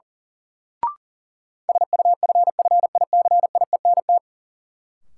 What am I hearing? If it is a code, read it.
Morse code, heard as a keyed tone, about 37 wpm, 700 Hz. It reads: SUFFICIENT
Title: Sentences from Top 1400 Words - New Word Drill Alt - 40wpm